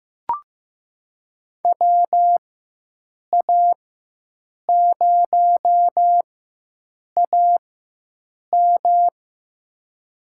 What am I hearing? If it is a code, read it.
Morse code, heard as a keyed tone, about 15 wpm, spaced as for 7 wpm, 700 Hz. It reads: WA0AM
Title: USA Call Signs - 4x Character Spacing - 15wpm